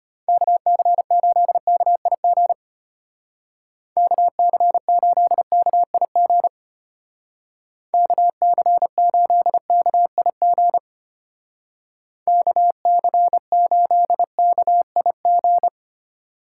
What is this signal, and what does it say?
Morse code, heard as a keyed tone, about 33 wpm, 700 Hz. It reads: KC8KIG KC8KIG KC8KIG KC8KIG